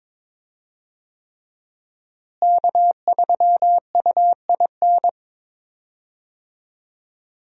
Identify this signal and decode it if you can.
Morse code, heard as a keyed tone, about 21 wpm, 700 Hz. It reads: K3UIN
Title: USA Call Signs - 5 Characters - Repeated 3x - 22wpm